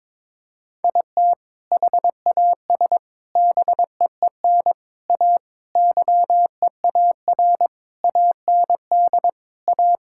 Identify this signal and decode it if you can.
Morse code, heard as a keyed tone, about 22 wpm, 700 Hz. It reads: IT HAS BEEN A YEAR AND A
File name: practice / 22wpm / Sentences with 4-Word Phrases - 22wpm